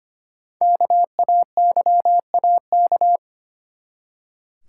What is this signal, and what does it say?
Morse code, heard as a keyed tone, about 25 wpm, 700 Hz. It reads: KAYAK
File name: practice / 25wpm / Easy Words with Hints - 25wpm